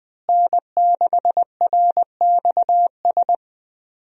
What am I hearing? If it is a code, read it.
Morse code, heard as a keyed tone, about 20 wpm, 700 Hz. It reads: N6RXS
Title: USA Call Signs - 5 Characters - Repeated 3x - 20wpm